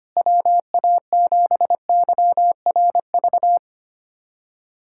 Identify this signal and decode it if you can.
Morse code, heard as a keyed tone, about 25 wpm, 700 Hz. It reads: WA7YRV
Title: USA Callsign DE Callsign 25wpm